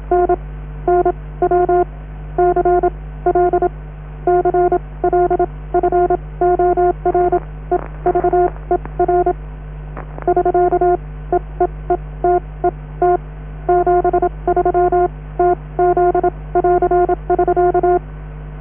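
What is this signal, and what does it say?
Morse code, heard as a keyed tone, about 27 wpm, spaced as for 19 wpm, 720 Hz, with noise. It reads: NNW CL CL FOREVER <SK> E E E T E T 73 TZ <AR><SK>
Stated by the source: Norddeich Radio (DAN) coast station Morse on 500 kHz.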